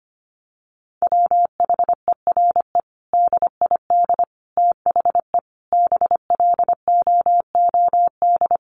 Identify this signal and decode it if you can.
Morse code, heard as a keyed tone, about 25 wpm, 700 Hz. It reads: WHERE DID THE BLOOD
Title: Sentences from Top 1100 Words 25wpm